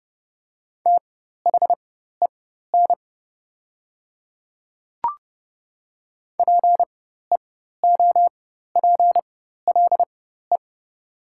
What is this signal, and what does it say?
Morse code, heard as a keyed tone, about 30 wpm, spaced as for 14 wpm, 700 Hz. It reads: THEN PEOPLE
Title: 4x Character Spacing - Top 500 Words - 30wpm